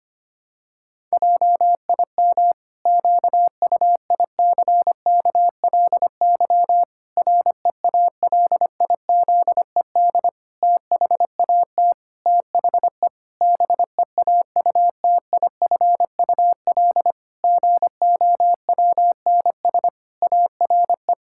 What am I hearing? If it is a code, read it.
Morse code, heard as a keyed tone, about 25 wpm, 700 Hz. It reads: JIM QUICKLY REALIZED THAT THE BEAUTIFUL GOWNS ARE
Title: Pangram Challenge v2 - 25wpm